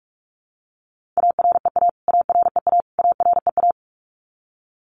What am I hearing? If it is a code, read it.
Morse code, heard as a keyed tone, about 44 wpm, 700 Hz. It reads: AREA AREA AREA
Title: Top 500 Words Repeated 45wpm